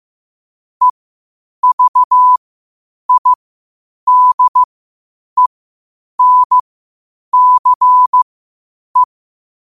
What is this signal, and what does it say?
Morse code, heard as a keyed tone, about 15 wpm, spaced as for 8 wpm, 1000 Hz. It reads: EVIDENCE
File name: qotdfiles / 8wpm